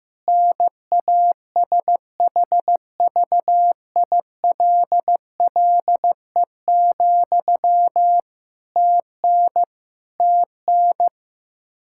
Morse code, heard as a keyed tone, about 15 wpm, 700 Hz. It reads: NASHVILLE, TN TN
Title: Practice QSO - Exchange 1 - Beginning - 15wpm